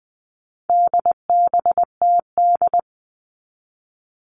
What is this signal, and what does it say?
Morse code, heard as a keyed tone, about 20 wpm, 700 Hz. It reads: DBTD
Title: All Cut Numbers - 4-Digit 20wpm